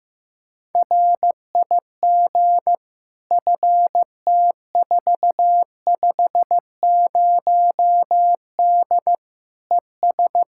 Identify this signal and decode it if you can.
Morse code, heard as a keyed tone, about 16 wpm, 700 Hz. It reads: RIG FT450D ES
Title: Practice QSO - Exchange 2 - Info - 15wpm